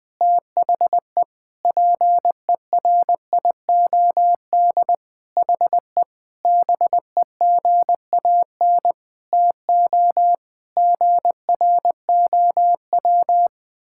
Morse code, heard as a keyed tone, about 20 wpm, 700 Hz. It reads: THE PERIOD HE BEGAN TO GROW
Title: Sentences from Top 1000 Words - Review Part 5 - 20wpm